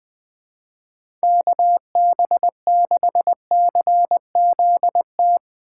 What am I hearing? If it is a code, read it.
Morse code, heard as a keyed tone, about 20 wpm, 700 Hz. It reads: KB6CZT
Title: USA Call Signs - 6 Characters - 4x Character Spacing - 20wpm